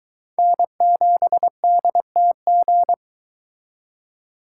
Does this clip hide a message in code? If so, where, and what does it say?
Yes: Morse code from 0.4 to 2.9 s, heard as a keyed tone, about 23 wpm, 700 Hz. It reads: N7DTG